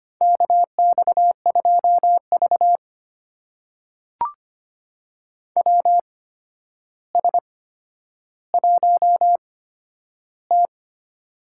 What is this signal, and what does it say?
Morse code, heard as a keyed tone, about 25 wpm, spaced as for 11 wpm, 700 Hz. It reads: KX2V WS1T
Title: USA Call Signs - 4 Characters - 8x Character Spacing - 25wpm